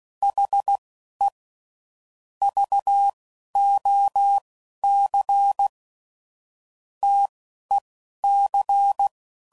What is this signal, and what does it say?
Morse code, heard as a keyed tone, about 16 wpm, 800 Hz. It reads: H E V O C T E C
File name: CW Code Groups @15WPM / QSOs / WD0BGP Spaced